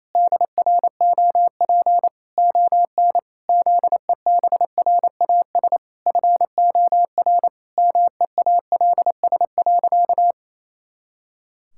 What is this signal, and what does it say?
Morse code, heard as a keyed tone, about 28 wpm, 700 Hz. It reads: DROP ON ZEBRAS FOR MEALS.
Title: Pangram Challenge v2 - 28wpm